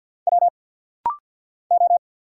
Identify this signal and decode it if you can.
Morse code, heard as a keyed tone, about 50 wpm, 700 Hz. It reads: W O
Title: Single Letters - Rapid-Fire 50wpm